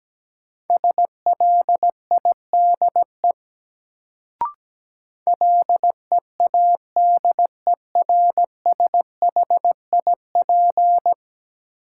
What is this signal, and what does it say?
Morse code, heard as a keyed tone, about 17 wpm, 700 Hz. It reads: SLIDE LEADERSHIP
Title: Sentences from Top 1600 Words - New Word Drill Alt - 17wpm